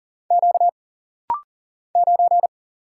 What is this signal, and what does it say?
Morse code, heard as a keyed tone, about 40 wpm, 700 Hz. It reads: Q 9
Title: Single Letter-Number Alt - Rapid-Fire 40wpm